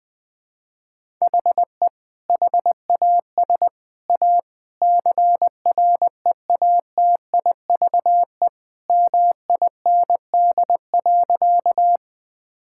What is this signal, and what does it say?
Morse code, heard as a keyed tone, about 20 wpm, 700 Hz. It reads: HE HAS A CREATIVE MIND.
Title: Sentences from Top 1000 Words - Review Part 5 - 20wpm